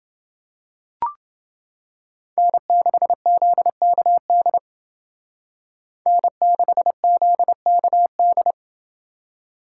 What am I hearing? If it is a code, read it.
Morse code, heard as a keyed tone, about 30 wpm, 700 Hz. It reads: N6ZKD N6ZKD